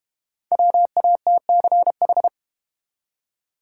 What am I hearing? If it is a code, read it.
Morse code, heard as a keyed tone, about 32 wpm, 700 Hz. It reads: WATCH